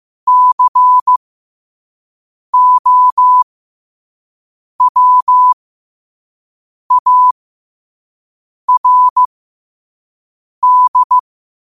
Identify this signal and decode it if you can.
Morse code, heard as a keyed tone, about 15 wpm, spaced as for 5 wpm, 1000 Hz. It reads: COWARD
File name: qotdfiles / 5wpm